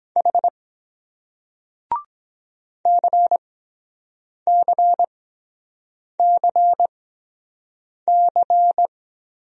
Morse code, heard as a keyed tone, about 23 wpm, spaced as for 7 wpm, 700 Hz. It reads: H CCCC